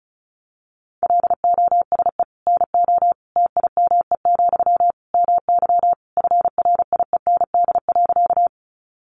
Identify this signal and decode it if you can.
Morse code, heard as a keyed tone, about 34 wpm, 700 Hz. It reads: LOSE NO TIME, MY FRIEND.
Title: Sentences from Top 1100 Words 35wpm